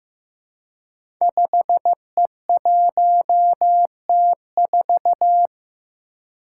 Morse code, heard as a keyed tone, about 15 wpm, 700 Hz. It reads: 5E1T4
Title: Five Challenging Letter-Number Combinations 15wpm